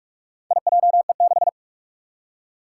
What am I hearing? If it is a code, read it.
Morse code, heard as a keyed tone, about 46 wpm, 700 Hz. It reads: IJE6
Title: Four Challenging Letter-Number Combinations 45wpm